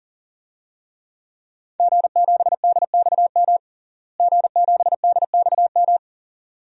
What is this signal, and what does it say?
Morse code, heard as a keyed tone, about 40 wpm, 700 Hz. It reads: G7DXM G7DXM
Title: DXCC-100 Call Signs - 5 Characters - Repeated 2x - 40wpm